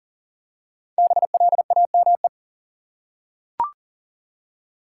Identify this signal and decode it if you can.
Morse code, heard as a keyed tone, about 37 wpm, 700 Hz. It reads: BLAME
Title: Five Letter Words - 40wpm